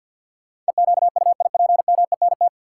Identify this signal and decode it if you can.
Morse code, heard as a keyed tone, about 50 wpm, 700 Hz. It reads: EQUIPMENT
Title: Sentences from Top 700 Words - New Word Drill - 50wpm